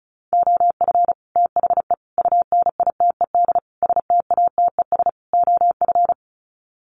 Morse code, heard as a keyed tone, about 34 wpm, 700 Hz. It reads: OF THE UNITED STATES OF
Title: Sets of 5 Words Encore 35wpm